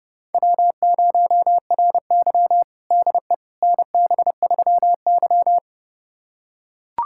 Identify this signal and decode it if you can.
Morse code, heard as a keyed tone, about 30 wpm, 700 Hz. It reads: W0RY DE NB3Y
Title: USA Callsign DE Callsign - 4 Characters - 30wpm